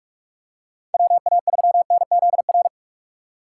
Morse code, heard as a keyed tone, about 45 wpm, 700 Hz. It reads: WA3NZR
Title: K1USN SST QSO Practice - 45wpm